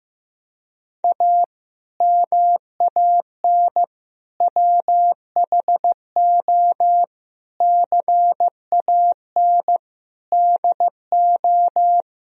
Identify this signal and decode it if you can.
Morse code, heard as a keyed tone, about 15 wpm, 700 Hz. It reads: A MAN WHO CAN DO
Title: Sentences from Top 1000 Words - Review Part 2 - 15wpm